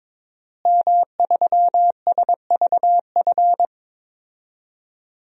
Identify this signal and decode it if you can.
Morse code, heard as a keyed tone, about 22 wpm, 700 Hz. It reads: M3SVF
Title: DXCC-100 Call Signs - 5 Characters - Repeated 2x - 22wpm